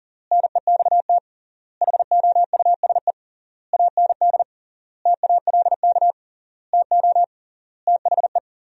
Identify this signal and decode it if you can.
Morse code, heard as a keyed tone, about 37 wpm, 700 Hz. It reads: NEXT HOUSE AND TALK TO THE
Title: 3x Word-Spacing - Sentences from Top 400 Words - 40wpm